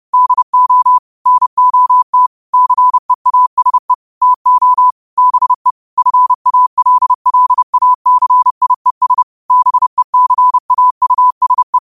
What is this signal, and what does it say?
Morse code, heard as a keyed tone, about 30 wpm, 1000 Hz. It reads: DO NOT CEASE TO BE FALLACIES BECAUSE
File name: qotdfiles / 30wpm